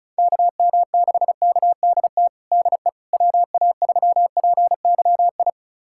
Morse code, heard as a keyed tone, about 34 wpm, 700 Hz. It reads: KM6KDT DE WA3PYI
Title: USA Callsign DE Callsign - 6 Characters - 35wpm